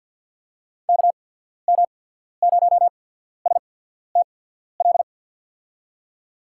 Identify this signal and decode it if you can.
Morse code, heard as a keyed tone, about 53 wpm, spaced as for 14 wpm, 700 Hz. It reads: KM0STL
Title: USA Call Signs - 6 Characters - 8x Character Spacing - 50wpm